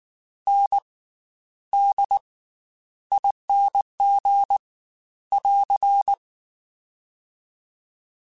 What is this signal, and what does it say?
Morse code, heard as a keyed tone, about 19 wpm, 800 Hz. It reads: N D ING <AR>
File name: Special INGs / Special INGs-mp3s / Slow ing @18